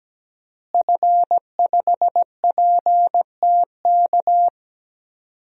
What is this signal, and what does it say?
Morse code, heard as a keyed tone, about 17 wpm, 700 Hz. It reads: F5PTK
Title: DXCC-100 Call Signs - 5 Characters - Repeated 2x - 17wpm